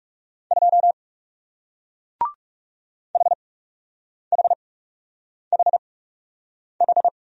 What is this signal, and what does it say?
Morse code, heard as a keyed tone, about 41 wpm, spaced as for 8 wpm, 700 Hz. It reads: 2 HHHH